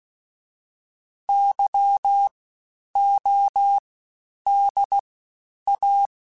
Morse code, heard as a keyed tone, about 16 wpm, spaced as for 9 wpm, 800 Hz. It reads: YODA
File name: Special INGs-mp3s / Special INGs-mp3s / Fast ing @15